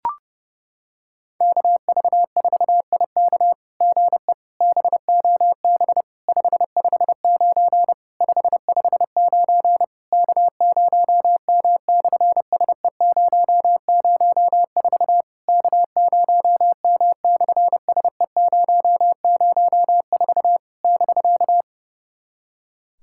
Morse code, heard as a keyed tone, about 30 wpm, 690 Hz. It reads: KV4IK GE BOB 559 559 K0M/SE004 K0M/SE004 <BK>
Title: SOTA QSO Practice 30wpm